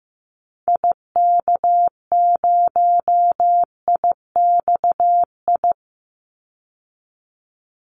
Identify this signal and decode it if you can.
Morse code, heard as a keyed tone, about 15 wpm, 700 Hz. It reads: IK0IXI
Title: Call Signs Repeated 3x - 15wpm